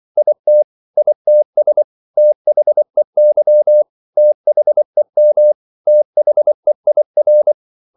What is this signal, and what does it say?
Morse code, heard as a keyed tone, about 24 wpm, 600 Hz. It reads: IT ITS THEY THEM THEIR
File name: CW / Practice / 100MostCommonEnglishWords24